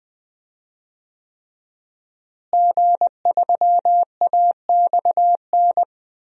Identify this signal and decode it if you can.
Morse code, heard as a keyed tone, about 20 wpm, 700 Hz. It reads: G3AXN
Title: DXCC-100 Call Signs - 5 Characters - Repeated 3x - 20wpm